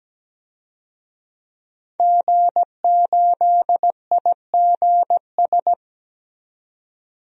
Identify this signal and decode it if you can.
Morse code, heard as a keyed tone, about 17 wpm, 700 Hz. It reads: G8IGS
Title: DXCC-100 Call Signs - 5 Characters - 17wpm